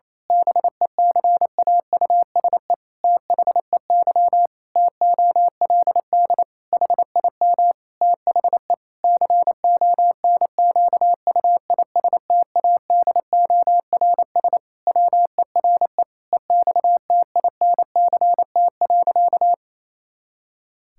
Morse code, heard as a keyed tone, about 28 wpm, 700 Hz. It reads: BECAUSE THEY TOLD HIM THE CONQUISTADORS WERE EXTINCT.
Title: Pangram Challenge v2 - 28wpm